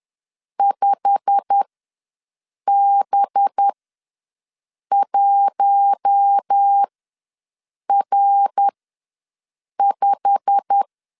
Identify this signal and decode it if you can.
Morse code, heard as a keyed tone, about 10 wpm, spaced as for 6 wpm, 800 Hz. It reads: 5B1R5
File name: LEZIONI / 8 YJ45 / 8 YJ45 ws 10wpm 3843